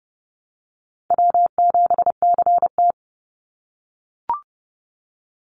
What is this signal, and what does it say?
Morse code, heard as a keyed tone, about 30 wpm, 700 Hz. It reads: W7CT
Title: Call Signs - 30wpm